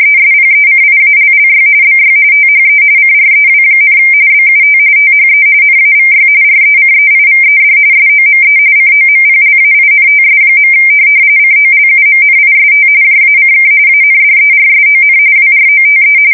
RTTY, about 45.45 baud. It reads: ESPECIALLY BY THE CREW, BUT THE NAVY RECEIVED SOME COMPLAINTS. COLUMNIST JACK ANDERSON WROTE
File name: fsk170 / USS-Missouri